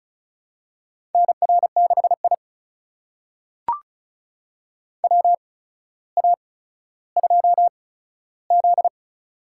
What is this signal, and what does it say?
Morse code, heard as a keyed tone, about 36 wpm, spaced as for 15 wpm, 700 Hz. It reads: NR6I WA2Z
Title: USA Call Signs - 4 Characters - 8x Character Spacing - 35wpm